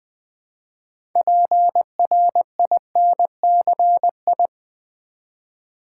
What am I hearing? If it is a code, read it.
Morse code, heard as a keyed tone, about 20 wpm, 700 Hz. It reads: PRINCI
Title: Build A Word - 3 Letters At A Time - 9-Letter Words - 20wpm